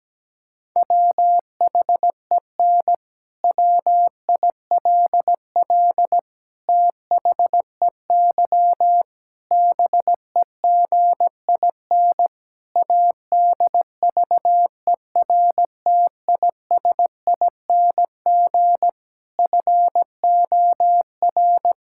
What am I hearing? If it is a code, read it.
Morse code, heard as a keyed tone, about 17 wpm, 700 Hz. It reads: WHEN WILL THEY BEGIN ADVERTISING FOR
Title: Sentences from Top 1000 Words - Review Part 6 - 17wpm